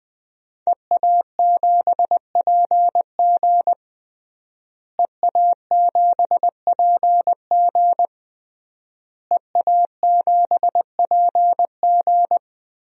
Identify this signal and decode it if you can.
Morse code, heard as a keyed tone, about 20 wpm, 700 Hz. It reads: EA7PG EA7PG EA7PG
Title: DXCC-100 Call Signs - 5 Characters - Repeated 3x - 20wpm